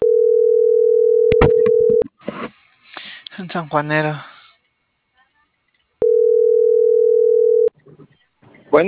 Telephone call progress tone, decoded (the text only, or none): ringback tone